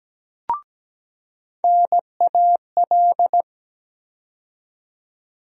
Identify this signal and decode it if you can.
Morse code, heard as a keyed tone, about 17 wpm, 700 Hz. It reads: NAL
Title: USA Call Sign Postfixes - 3 Letters 17wpm